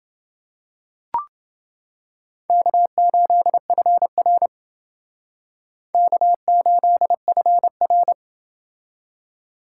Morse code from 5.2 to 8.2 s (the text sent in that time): K8FR